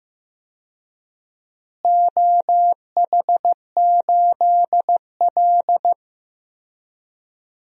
Morse code, heard as a keyed tone, about 15 wpm, 700 Hz. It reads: OH8L
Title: DXCC-100 Call Signs - 4 Characters - Repeated 3x - 15wpm